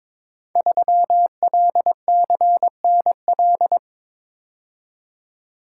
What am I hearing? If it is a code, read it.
Morse code, heard as a keyed tone, about 22 wpm, 700 Hz. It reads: 3LCNL
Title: Code Groups - 5 Letter:Numbers - 22wpm